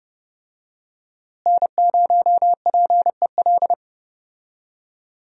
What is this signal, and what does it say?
Morse code, heard as a keyed tone, about 32 wpm, 700 Hz. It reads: N0PEL
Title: USA Call Signs - 5 Characters - Repeated 3x - 30wpm